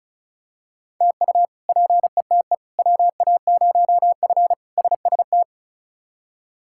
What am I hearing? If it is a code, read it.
Morse code, heard as a keyed tone, about 34 wpm, 700 Hz. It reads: TU PETE WA0F SST
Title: K1USN SST QSO Practice - 35wpm